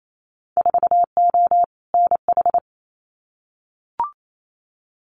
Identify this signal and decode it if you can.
Morse code, heard as a keyed tone, about 28 wpm, 700 Hz. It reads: 4O NH
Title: Winter Field Day Exchanges Encore 28wpm